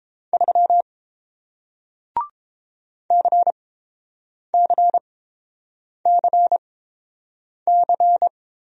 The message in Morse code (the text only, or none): 3 CCCC